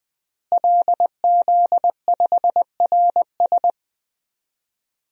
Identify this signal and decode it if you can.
Morse code, heard as a keyed tone, about 20 wpm, 700 Hz. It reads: LZ5RS
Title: DXCC-100 Call Signs - 5 Characters - Repeated 2x - 20wpm